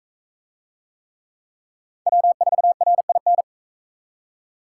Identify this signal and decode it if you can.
Morse code, heard as a keyed tone, about 38 wpm, 700 Hz. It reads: W4RIN